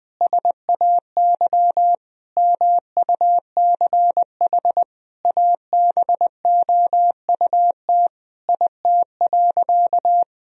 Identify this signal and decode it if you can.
Morse code, heard as a keyed tone, about 20 wpm, 700 Hz. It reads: SAY MUCH ABOUT IT.
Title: Sentences from Top 1000 Words - Review Part 2 - 20wpm